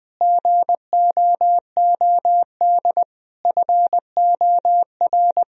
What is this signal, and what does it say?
Morse code, heard as a keyed tone, about 21 wpm, 700 Hz. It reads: GOOD FOR
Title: Sentences from Top 1000 Words - Review Part 3 - 20wpm